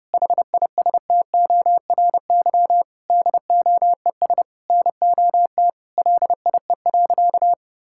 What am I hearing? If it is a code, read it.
Morse code, heard as a keyed tone, about 30 wpm, 700 Hz. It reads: HISTORY DOES NOT LIE.
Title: Sentences from Top 1600 Words 30wpm